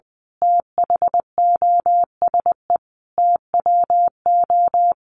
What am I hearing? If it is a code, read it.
Morse code, heard as a keyed tone, about 20 wpm, 700 Hz. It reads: THOSE TWO
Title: A Sound Of Thunder 20wpm